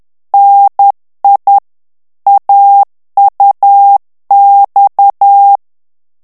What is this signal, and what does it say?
Morse code, heard as a keyed tone, about 10 wpm, 800 Hz. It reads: NI AUX
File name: morse mp3 / FRANCAIS TEXTE CLAIR 2 VITESSE 10 MOTS MINUTES